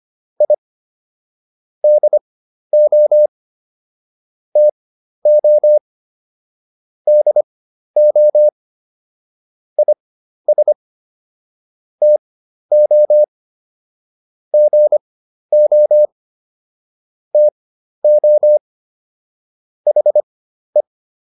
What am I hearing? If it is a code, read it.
Morse code, heard as a keyed tone, about 25 wpm, spaced as for 12 wpm, 600 Hz. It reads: I DO TO DO IS TO GO TO HE